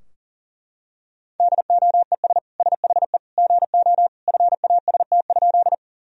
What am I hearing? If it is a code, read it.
Morse code, heard as a keyed tone, about 40 wpm, 700 Hz. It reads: DOES SHE GO FAST?
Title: Sentences from Top 1000 Words - Review Part 2 - 40wpm